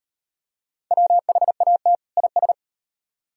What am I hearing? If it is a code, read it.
Morse code, heard as a keyed tone, about 38 wpm, 700 Hz. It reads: WHAT IS